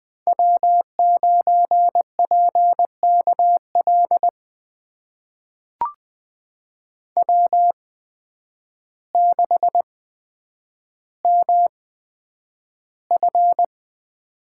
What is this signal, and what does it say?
Morse code, heard as a keyed tone, about 20 wpm, spaced as for 9 wpm, 700 Hz. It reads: W9PKL W6MF
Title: USA Call Signs - 8x Character Spacing - 20wpm